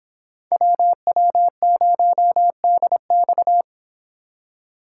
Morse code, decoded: WW0DX

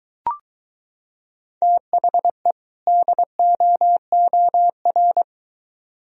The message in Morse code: THE DOOR

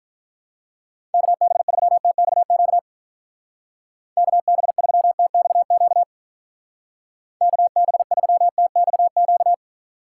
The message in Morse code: KB3TXQ KB3TXQ KB3TXQ

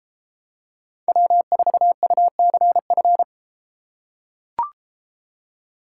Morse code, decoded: W4UCF

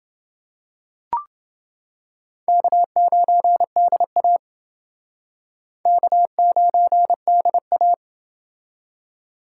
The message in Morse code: K9DA K9DA

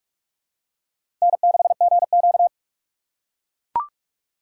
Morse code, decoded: N6GQ

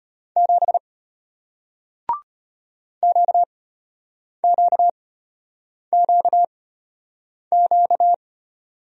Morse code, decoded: 7 QQQQ

